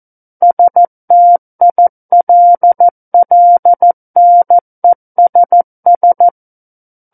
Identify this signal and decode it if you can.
Morse code, heard as a keyed tone, about 14 wpm, 700 Hz. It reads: STILLNESS
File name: Book One / 05-Chapter 5 - 14 WPM